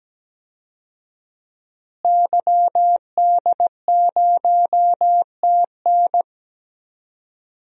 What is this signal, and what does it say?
Morse code, heard as a keyed tone, about 17 wpm, 700 Hz. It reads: YD0TN
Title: DXCC-100 Call Signs - 5 Characters - Repeated 3x - 17wpm